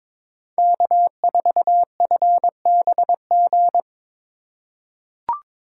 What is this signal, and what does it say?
Morse code, heard as a keyed tone, about 22 wpm, 700 Hz. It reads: K4FBG
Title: USA Call Signs - 5 Characters - 3x Character Spacing - 22wpm